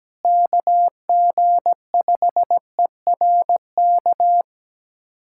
Morse code, heard as a keyed tone, about 17 wpm, 700 Hz. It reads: KG5ERK
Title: USA Callsign DE Callsign - 6 Characters - 17wpm